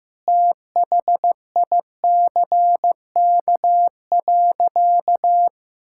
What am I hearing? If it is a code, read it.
Morse code, heard as a keyed tone, about 15 wpm, 700 Hz. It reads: THICK.
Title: Sentences from Top 1600 Words 15wpm